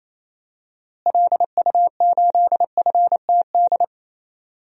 Morse code, heard as a keyed tone, about 28 wpm, 700 Hz. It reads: LU8FTD
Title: DXCC-100 Call Signs - 6 Characters - Repeated 2x - 28wpm